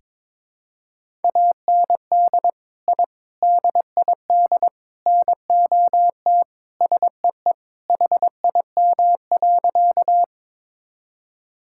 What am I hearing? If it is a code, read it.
Morse code, heard as a keyed tone, about 22 wpm, 700 Hz. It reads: AND I DID NOT SEE HIM.
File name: practice / 22wpm / Sentences with 4-Word Phrases - 22wpm